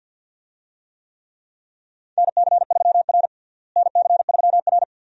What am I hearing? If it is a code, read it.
Morse code, heard as a keyed tone, about 52 wpm, 700 Hz. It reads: NC3R NC3R
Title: USA Call Signs - 4 Characters - Repeated 2x - 50wpm